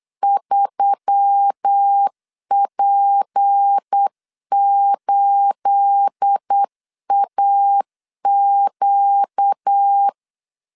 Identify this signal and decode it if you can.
Morse code, heard as a keyed tone, about 8 wpm, 800 Hz. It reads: 3P8AQ
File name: LEZIONI / 9 28 8wpm 3512